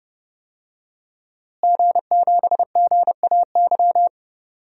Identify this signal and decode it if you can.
Morse code, heard as a keyed tone, about 30 wpm, 700 Hz. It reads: G7GAY
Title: DXCC-100 Call Signs - 5 Characters - Repeated 3x - 30wpm